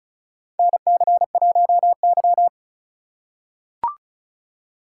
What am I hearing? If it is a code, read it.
Morse code, heard as a keyed tone, about 36 wpm, 690 Hz. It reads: NC1Y
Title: USA Call Signs - 4 Characters - 2x Character Spacing - 35wpm